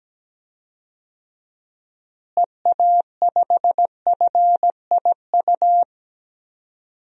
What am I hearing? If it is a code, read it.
Morse code, heard as a keyed tone, about 17 wpm, 700 Hz. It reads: EA5FIU
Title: DXCC-100 Call Signs - 6 Characters - Repeated 2x - 17wpm